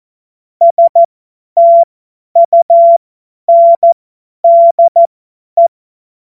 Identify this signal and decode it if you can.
Morse code, heard as a keyed tone, about 14 wpm, spaced as for 10 wpm, 680 Hz. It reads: STUNDE
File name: Witz-07-Oktober-2025-50-F70